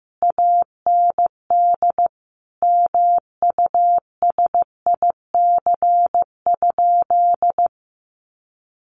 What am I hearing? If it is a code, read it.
Morse code, heard as a keyed tone, about 15 wpm, 700 Hz. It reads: AND MUSIC?
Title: Sentences from Top 1200 Words 15wpm